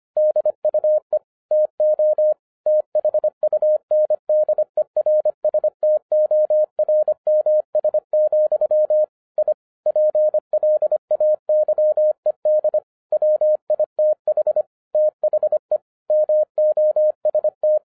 Morse code, heard as a keyed tone, about 25 wpm, 610 Hz. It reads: DUE TO THUNDERSTORMS, I PLAYED WITH THE MOST
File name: 2012-01-mp3 / n14-FUN WITH NUMBERS-25wpm